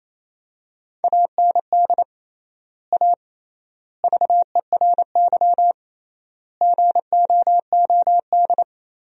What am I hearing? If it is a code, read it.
Morse code, heard as a keyed tone, about 28 wpm, 700 Hz. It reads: AND A VERY GOOD
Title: 3x Word-Spacing - Sentences from Top 200 Words - 28wpm